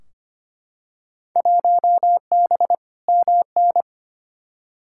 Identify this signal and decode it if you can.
Morse code, heard as a keyed tone, about 25 wpm, 700 Hz. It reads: 1B MN
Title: Field Day Exchanges Encore - 25wpm